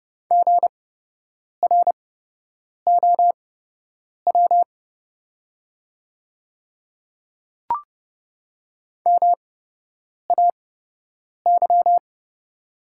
Morse code, heard as a keyed tone, about 30 wpm, spaced as for 8 wpm, 700 Hz. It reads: GROW MAY
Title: 8x Character Spacing - Top 300 Words - 30wpm